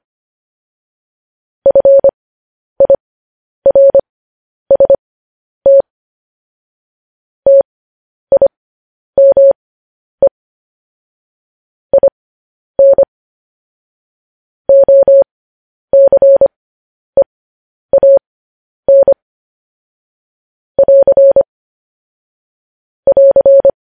Morse code, heard as a keyed tone, about 25 wpm, spaced as for 10 wpm, 550 Hz. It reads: FIRST TIME IN OCEAN <AR> <AR>
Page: News Headlines in Morse Code at 10 WPM